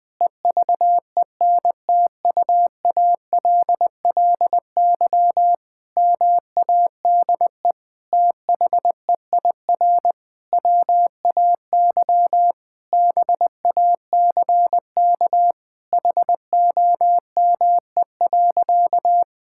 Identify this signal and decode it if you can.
Morse code, heard as a keyed tone, about 20 wpm, 700 Hz. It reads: EVENTUALLY MADE THEIR WAY BACK HOME.
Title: Sentences from Top 1000 Words - Review Part 6 - 20wpm